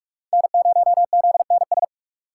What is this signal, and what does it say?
Morse code, heard as a keyed tone, about 46 wpm, 700 Hz. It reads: N0ZNS
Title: USA Call Signs - 5 Characters - 6x Character Spacing - 45wpm